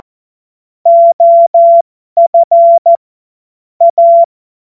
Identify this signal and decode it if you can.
Morse code, heard as a keyed tone, about 14 wpm, spaced as for 12 wpm, 680 Hz. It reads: OF A
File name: Morsekurs / Witze